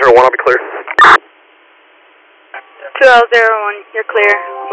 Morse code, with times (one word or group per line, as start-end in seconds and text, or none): none